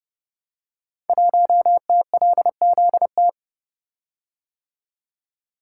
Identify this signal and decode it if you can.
Morse code, heard as a keyed tone, about 30 wpm, 700 Hz. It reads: 1TLZT
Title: Code Groups - 5 Letter:Numbers - 30wpm